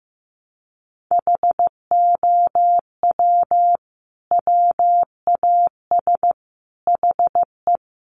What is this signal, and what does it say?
Morse code, heard as a keyed tone, about 15 wpm, 700 Hz. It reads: HOW WAS HE
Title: Sentences from Top 1400 Words 15wpm